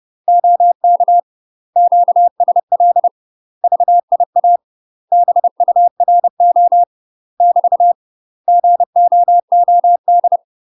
Morse code, heard as a keyed tone, about 30 wpm, 700 Hz. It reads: OK QSL VIA BURO <BT> GOOD
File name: UK MASTER QSOs TRAINING / 30WPM / UK MASTER QSOs 0042